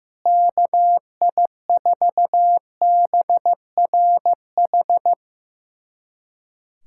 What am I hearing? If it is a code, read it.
Morse code, heard as a keyed tone, about 15 wpm, 700 Hz. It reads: KI4BRH
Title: USA Callsign DE Callsign 15wpm